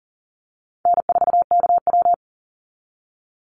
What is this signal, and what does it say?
Morse code, heard as a keyed tone, about 38 wpm, 700 Hz. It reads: N4KW
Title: Call Signs - 40wpm